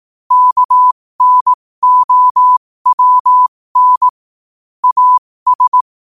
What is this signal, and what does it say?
Morse code, heard as a keyed tone, about 18 wpm, spaced as for 15 wpm, 1000 Hz. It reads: KNOWN AS
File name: qotdfiles / 15wpm